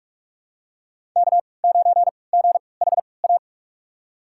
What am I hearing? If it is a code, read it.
Morse code, heard as a keyed tone, about 46 wpm, spaced as for 26 wpm, 700 Hz. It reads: K9GHA
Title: USA Call Signs - 3x Character Spacing - 45wpm